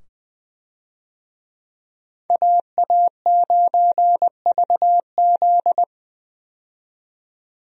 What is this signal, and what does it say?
Morse code, heard as a keyed tone, about 20 wpm, 700 Hz. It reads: AA9VZ
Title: USA Call Signs - 5 Characters - 20wpm